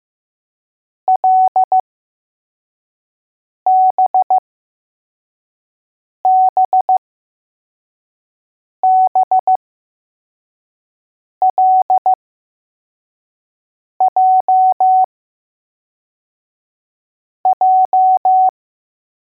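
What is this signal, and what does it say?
Morse code, heard as a keyed tone, about 15 wpm, spaced as for 4 wpm, 750 Hz. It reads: LBBBLJJ
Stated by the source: Morse code letters test drill for L, J and B.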